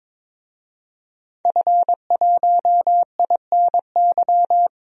Morse code, heard as a keyed tone, about 21 wpm, 700 Hz. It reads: F1INY
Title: DXCC-100 Call Signs - 5 Characters - 22wpm